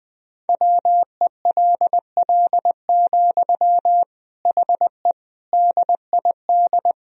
Morse code, heard as a keyed tone, about 20 wpm, 700 Hz. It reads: WELL, HE DID
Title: Sentences from Top 1000 Words - Review Part 2 - 20wpm